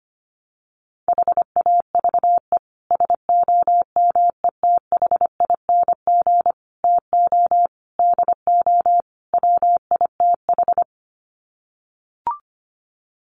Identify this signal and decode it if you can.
Morse code, heard as a keyed tone, about 25 wpm, 700 Hz. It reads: HAVE SOMETHING TO DO WITH
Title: Sets of 5 Words Encore 25wpm